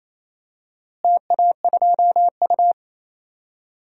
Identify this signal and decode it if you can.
Morse code, heard as a keyed tone, about 28 wpm, 700 Hz. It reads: TA2U
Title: DXCC-100 Call Signs - 4 Characters - Repeated 2x - 28wpm